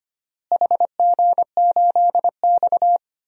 Morse code, heard as a keyed tone, about 25 wpm, 700 Hz. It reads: HG8X
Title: DXCC-100 Call Signs - 4 Characters - Repeated 2x - 25wpm